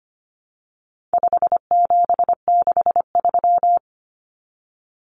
Morse code, heard as a keyed tone, about 25 wpm, 700 Hz. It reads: 5763
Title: Common Cut Numbers - 4-Digit 25wpm